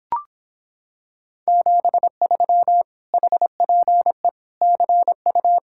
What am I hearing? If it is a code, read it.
Morse code, heard as a keyed tone, about 26 wpm, 700 Hz. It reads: 73 HPE CU